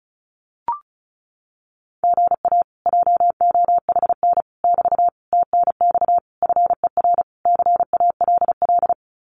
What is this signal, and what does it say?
Morse code, heard as a keyed tone, about 36 wpm, 700 Hz. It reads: GA JOHN <BT> TNX FER CALL
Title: Common QSO Phrases Encore 35wpm